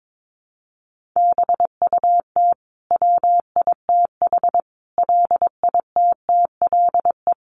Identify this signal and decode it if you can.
Morse code, heard as a keyed tone, about 22 wpm, 700 Hz. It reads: BUT WITH LITTLE
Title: Sentences from Top 1400 Words 22wpm